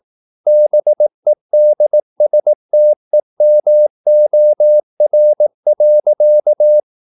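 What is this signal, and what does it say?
Morse code, heard as a keyed tone, about 18 wpm, 600 Hz. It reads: BEDSTEMOR.
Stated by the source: Slow (QRS) Morse practice at 18 wpm, CW med Gnister.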